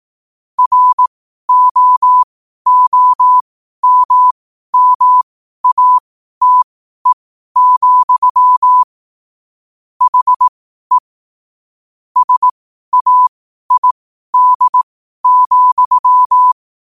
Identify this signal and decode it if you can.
Morse code, heard as a keyed tone, about 18 wpm, spaced as for 12 wpm, 1000 Hz. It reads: ROOMMATE, HE SAID,
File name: qotdfiles / 12wpm